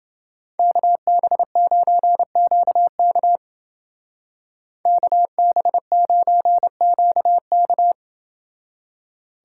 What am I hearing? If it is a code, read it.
Morse code, heard as a keyed tone, about 29 wpm, 700 Hz. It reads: KB9QK KB9QK